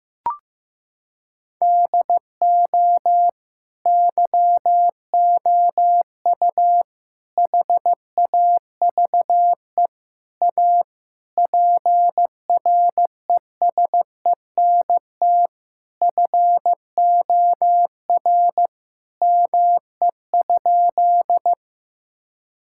Morse code, heard as a keyed tone, about 15 wpm, 700 Hz. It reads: DO YOU HAVE A PRESENT FOR ME?
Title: Sentences from Top 1000 Words - Review Part 5 - 15wpm